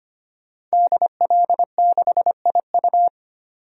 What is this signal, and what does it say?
Morse code, heard as a keyed tone, about 25 wpm, 700 Hz. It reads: DL6IU
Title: DXCC-100 Call Signs - 5 Characters - 25wpm